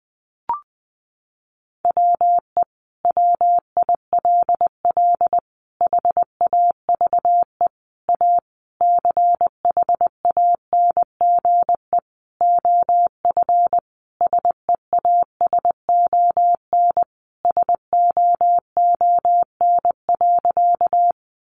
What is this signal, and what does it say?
Morse code, heard as a keyed tone, about 20 wpm, 700 Hz. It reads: WE WILL HAVE A CHANGE OF SEASON SOON.
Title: Sentences from Top 1400 Words 20wpm